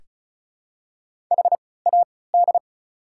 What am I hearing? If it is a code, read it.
Morse code, heard as a keyed tone, about 34 wpm, spaced as for 20 wpm, 690 Hz. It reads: HAD
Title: 3x Character Spacing - Top 200 Words - 35wpm